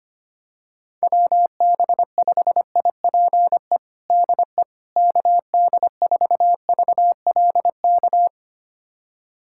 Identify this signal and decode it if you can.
Morse code, heard as a keyed tone, about 25 wpm, 700 Hz. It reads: WB5IPE DE KD4VLK
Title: USA Callsign DE Callsign 25wpm